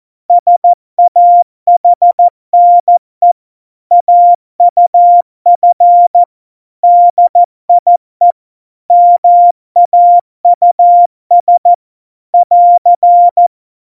Morse code, heard as a keyed tone, about 14 wpm, 700 Hz. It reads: SAHNE AUF DIE MAUS <AR>
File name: Morsekurs / Witze / Witz 03 April 2018, 70 F80